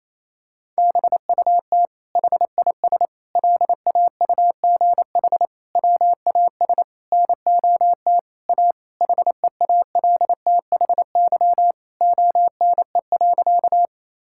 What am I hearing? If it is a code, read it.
Morse code, heard as a keyed tone, about 28 wpm, 700 Hz. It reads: BUT HIS LAUGH WAS NOT A HEALTHY ONE.
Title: Sentences from Top 1000 Words - Review Part 5 - 28wpm